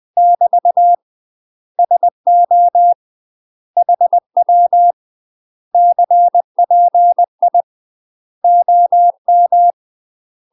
Morse code, heard as a keyed tone, about 20 wpm, 700 Hz. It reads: <BT> SO HW CPI OM
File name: UK MASTER QSOs TRAINING/20WPM